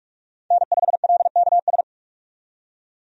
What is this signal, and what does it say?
Morse code, heard as a keyed tone, about 46 wpm, 700 Hz. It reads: N5LKS
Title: USA Call Signs - 5 Characters - 45wpm